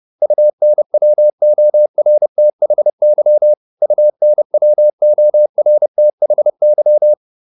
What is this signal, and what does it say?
Morse code, heard as a keyed tone, about 30 wpm, 600 Hz. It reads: UNWORTHY UNWORTHY